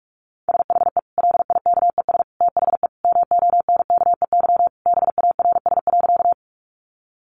Morse code, heard as a keyed tone, about 44 wpm, 700 Hz. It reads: SHE LIKES THE MONKEY BARS.
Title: Easy Phrases 45wpm